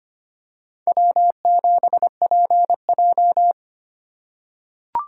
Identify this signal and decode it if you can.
Morse code, heard as a keyed tone, about 25 wpm, 700 Hz. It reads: W7PJ
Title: USA Call Signs - 4 Characters - 3x Character Spacing - 25wpm